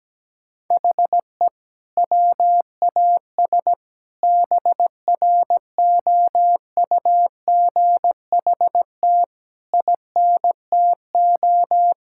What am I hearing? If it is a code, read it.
Morse code, heard as a keyed tone, about 17 wpm, 700 Hz. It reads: HE WAS BROUGHT INTO
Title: Sentences from Top 1600 Words 17wpm